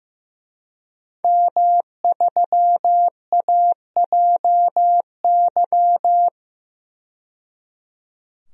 Morse code, heard as a keyed tone, about 15 wpm, 700 Hz. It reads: M3AJY
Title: DXCC-100 Call Signs - 5 Characters - 15wpm